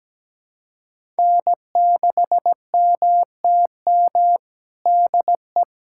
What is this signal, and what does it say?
Morse code, heard as a keyed tone, about 17 wpm, 700 Hz. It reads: N6MTM DE
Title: USA Callsign DE Callsign - 5 Characters - 17wpm